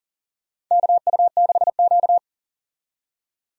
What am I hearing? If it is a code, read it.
Morse code, heard as a keyed tone, about 40 wpm, 700 Hz. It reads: KU6Q
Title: USA Call Signs - 4 Characters - 40wpm